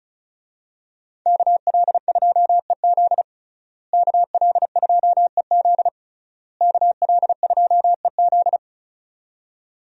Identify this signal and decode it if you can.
Morse code, heard as a keyed tone, about 36 wpm, 700 Hz. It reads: KL2EZ KL2EZ KL2EZ
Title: USA Call Signs - 5 Characters - Repeated 3x - 35wpm